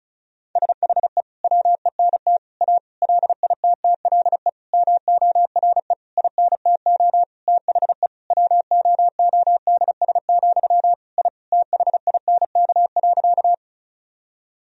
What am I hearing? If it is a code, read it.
Morse code, heard as a keyed tone, about 36 wpm, 700 Hz. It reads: SHE WENT A LITTLE MORE INTO THE WOODS, I THINK.
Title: Sentences from Top 1000 Words - Review Part 2 - 35wpm